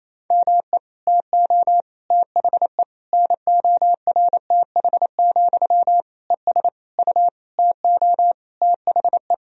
Morse code, heard as a keyed tone, about 28 wpm, 700 Hz. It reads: ME TO THE NORTH, ES U TO THE
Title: Sentences from Top 400 Words and Abbreviations 28wpm